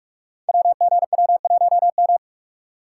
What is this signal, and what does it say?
Morse code, heard as a keyed tone, about 44 wpm, 700 Hz. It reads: WGW1M
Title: Code Groups - 5 Letter:Numbers - 45wpm